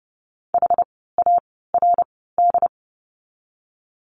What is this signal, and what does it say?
Morse code, heard as a keyed tone, about 30 wpm, 700 Hz. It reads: H A R D
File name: Lesson 16f - Letter c - Words 30wpm